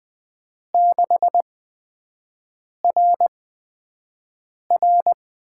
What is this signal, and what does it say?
Morse code, heard as a keyed tone, about 20 wpm, spaced as for 5 wpm, 700 Hz. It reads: 6RR
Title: USA Call Signs - 4 Characters - 8x Character Spacing - 20wpm